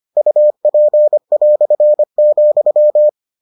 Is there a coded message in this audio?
Yes: Morse code from 0.2 to 3.1 s, heard as a keyed tone, about 25 wpm, 600 Hz. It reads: UP",